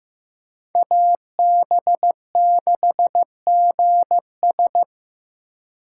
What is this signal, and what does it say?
Morse code, heard as a keyed tone, about 16 wpm, 700 Hz. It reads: AB6GS
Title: USA Call Signs - 5 Characters - 4x Character Spacing - 15wpm